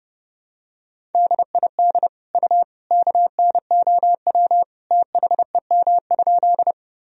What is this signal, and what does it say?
Morse code, heard as a keyed tone, about 31 wpm, 700 Hz. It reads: DID U KNOW THEM?
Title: Sentences from Top 100 Words and Abbreviations 30wpm